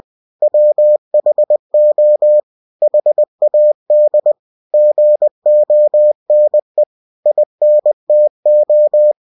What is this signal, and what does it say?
Morse code, heard as a keyed tone, about 20 wpm, 600 Hz. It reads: WHO HAD GONE INTO
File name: cw-books / winnie-the-pooh / winnie-20wpm-page-0037